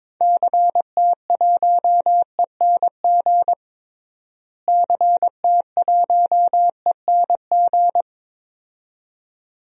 Morse code, heard as a keyed tone, about 21 wpm, 700 Hz. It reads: CT1ENG CT1ENG
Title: DXCC-100 Call Signs - 6 Characters - Repeated 2x - 22wpm